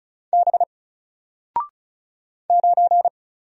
Morse code, heard as a keyed tone, about 36 wpm, 700 Hz. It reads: B 9
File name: Single-Letter-Number-35wpm